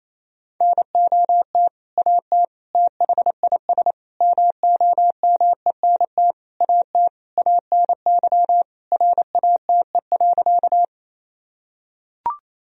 Morse code, heard as a keyed tone, about 28 wpm, 700 Hz. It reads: NOT AT THIS MOMENT AT ANY RATE.
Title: Sentences from Top 1000 Words - Review Part 4 - 28wpm